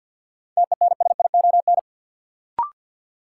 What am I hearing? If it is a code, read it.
Morse code, heard as a keyed tone, about 52 wpm, 700 Hz. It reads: TENSION
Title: Sentences from Top 1600 Words - New Word Drill - 50wpm